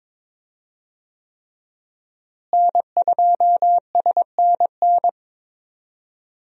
Morse code, heard as a keyed tone, about 23 wpm, 700 Hz. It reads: N2SNN
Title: USA Call Signs - 5 Characters - Repeated 3x - 22wpm